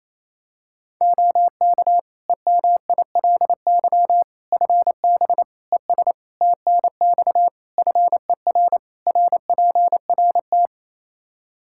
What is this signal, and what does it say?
Morse code, heard as a keyed tone, about 27 wpm, 700 Hz. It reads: OK EMILY FB ES TNX FER RPRT
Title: Practice QSO - Exchange 2 - Info - 28wpm